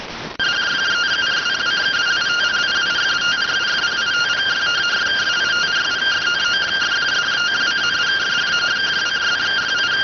Two-tone teleprinter signal, about 170 Hz shift.